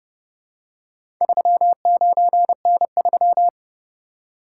Morse code, heard as a keyed tone, about 31 wpm, 700 Hz. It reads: 39N3